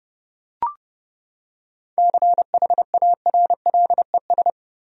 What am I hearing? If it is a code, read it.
Morse code, heard as a keyed tone, about 32 wpm, 700 Hz. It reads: CHARLES
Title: USA Common Names - 30wpm